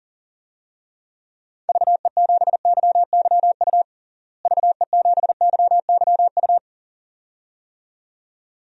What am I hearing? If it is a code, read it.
Morse code, heard as a keyed tone, about 40 wpm, 700 Hz. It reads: VE7YYU VE7YYU
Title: DXCC-100 Call Signs - 6 Characters - Repeated 2x - 40wpm